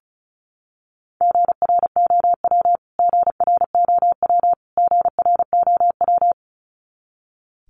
Morse code, heard as a keyed tone, about 34 wpm, 700 Hz. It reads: GROW GROW GROW
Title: Top 300 Words Repeated 35wpm